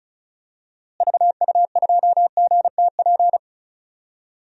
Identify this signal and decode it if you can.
Morse code, heard as a keyed tone, about 34 wpm, 700 Hz. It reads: VU2GTP
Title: DXCC-100 Call Signs - 6 Characters - Repeated 3x - 35wpm